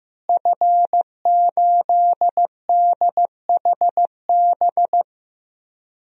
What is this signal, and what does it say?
Morse code, heard as a keyed tone, about 15 wpm, 700 Hz. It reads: F8DHB
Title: DXCC-100 Call Signs - 5 Characters - Repeated 3x - 15wpm